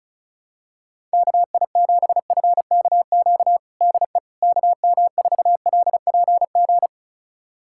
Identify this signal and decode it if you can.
Morse code, heard as a keyed tone, about 34 wpm, 690 Hz. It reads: KI7FKQ DE KM4LPG
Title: USA Callsign DE Callsign - 6 Characters - 35wpm